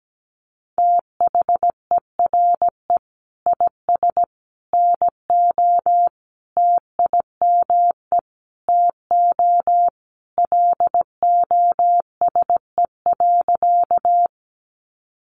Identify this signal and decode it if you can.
Morse code, heard as a keyed tone, about 17 wpm, 700 Hz. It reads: THERE IS NO TIME TO LOSE.
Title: Sentences from Top 1100 Words 17wpm